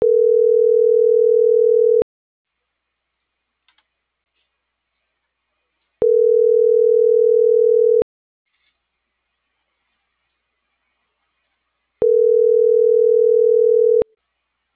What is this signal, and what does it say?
Ringback tone (North America), 440 + 480 Hz.